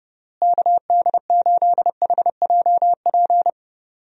Morse code, heard as a keyed tone, about 30 wpm, 700 Hz. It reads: KD8HJP
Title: USA Call Signs - 6 Characters - Repeated 2x - 30wpm